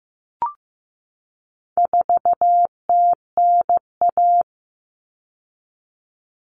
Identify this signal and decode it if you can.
Morse code, heard as a keyed tone, about 15 wpm, 700 Hz. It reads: 4TNA
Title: Common Cut Numbers - 4-Digit 15wpm